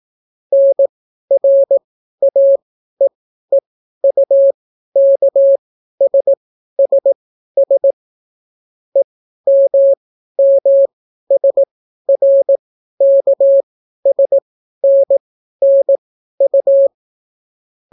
Morse code, heard as a keyed tone, about 18 wpm, spaced as for 12 wpm, 550 Hz. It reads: NRAEEUKSSS EMMSRKSNNU